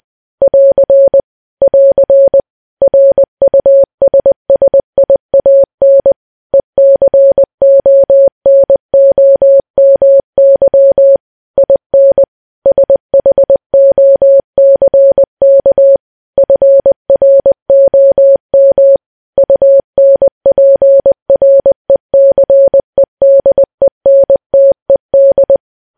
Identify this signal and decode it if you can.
Morse code, heard as a keyed tone, about 20 wpm, 550 Hz. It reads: <AR> <AR> RUSSIAN ECONOMY IN SHOCK FROM UNPRECEDENTED
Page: news-headlines-in-morse-code-at-20-wpm